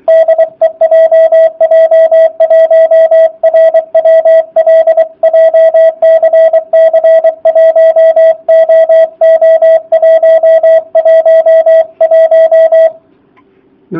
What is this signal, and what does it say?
Morse code, heard as a keyed tone, about 23 wpm, 650 Hz, with radio noise. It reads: DE JJ1RWL JCC1OO111